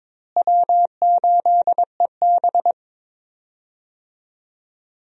Morse code, heard as a keyed tone, about 22 wpm, 700 Hz. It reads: W8EB
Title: USA Call Signs - 4 Characters - Repeated 2x - 22wpm